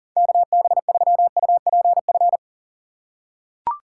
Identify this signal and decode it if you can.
Morse code, heard as a keyed tone, about 40 wpm, 700 Hz. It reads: KB3UPF
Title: USA Call Signs - 6 Characters - 6x Character Spacing - 40wpm